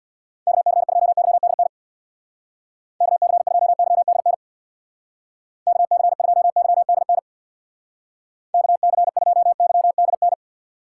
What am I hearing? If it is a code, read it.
Morse code, heard as a keyed tone, about 61 wpm, 700 Hz. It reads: KC2YDN KC2YDN KC2YDN KC2YDN
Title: Speed Racing - USA 2x3 Call Signs 50wpm